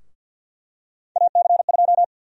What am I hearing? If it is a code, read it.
Morse code, heard as a keyed tone, about 52 wpm, 700 Hz. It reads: AC2
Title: USA Call Sign Prefixes - 50wpm